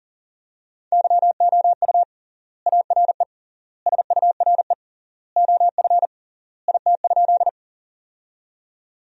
Morse code, heard as a keyed tone, about 40 wpm, 700 Hz. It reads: YOU ARE SURE OF IT?
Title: 3x Word-Spacing - Sentences from Top 300 Words - 40wpm